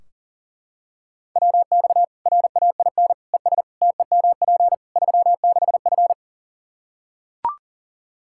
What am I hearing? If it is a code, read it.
Morse code, heard as a keyed tone, about 40 wpm, 700 Hz. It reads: WX RAIN ES TEMP 36F
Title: Practice QSO - Exchange 2 - Info - 40wpm